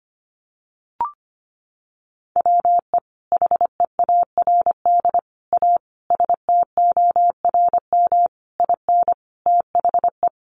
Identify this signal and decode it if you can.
Morse code, heard as a keyed tone, about 25 wpm, 700 Hz. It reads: WE HEARD A STORM IN THE
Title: Sentences from Top 1500 Words 25wpm